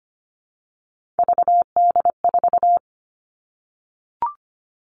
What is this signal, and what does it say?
Morse code, heard as a keyed tone, about 25 wpm, 700 Hz. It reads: VD4
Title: Three Challenging Letter-Number Combinations 25wpm